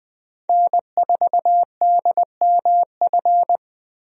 Morse code, heard as a keyed tone, about 21 wpm, 700 Hz. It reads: N4DMF
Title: USA Call Signs - 5 Characters - 20wpm